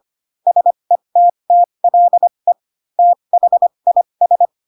Morse code, heard as a keyed tone, about 25 wpm, spaced as for 22 wpm, 700 Hz. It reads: SETTLE THIS
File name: Pinocchio / Pinocchio ch3 18 25 ik1hge